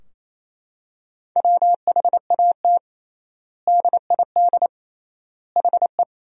Morse code, heard as a keyed tone, about 27 wpm, 700 Hz. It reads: WHAT DID HE
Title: 3x Word-Spacing - Sentences from Top 200 Words - 28wpm